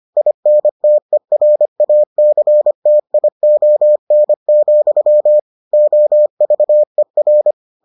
Morse code, heard as a keyed tone, about 25 wpm, 600 Hz. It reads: INTERACTION, OVER